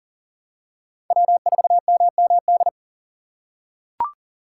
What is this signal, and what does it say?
Morse code, heard as a keyed tone, about 40 wpm, 700 Hz. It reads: W4MMD
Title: K1USN SST QSO Practice - 40wpm